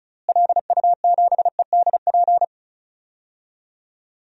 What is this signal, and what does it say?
Morse code, heard as a keyed tone, about 36 wpm, 700 Hz. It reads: LU7EDP
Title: DXCC-100 Call Signs - 6 Characters - Repeated 2x - 35wpm